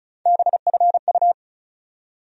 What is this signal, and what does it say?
Morse code, heard as a keyed tone, about 36 wpm, 700 Hz. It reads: BFU
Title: USA Call Sign Postfixes - 3 Letters 35wpm